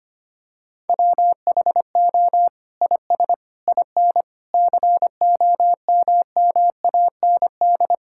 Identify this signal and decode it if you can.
Morse code, heard as a keyed tone, about 25 wpm, 700 Hz. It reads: WHO IS IN COMMAND
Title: Sentences from Top 1600 Words 25wpm